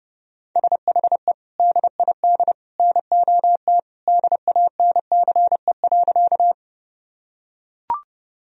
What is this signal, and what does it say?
Morse code, heard as a keyed tone, about 30 wpm, 700 Hz. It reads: SHE DID NOT DANCE.
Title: Sentences from Top 1000 Words - Review Part 4 - 30wpm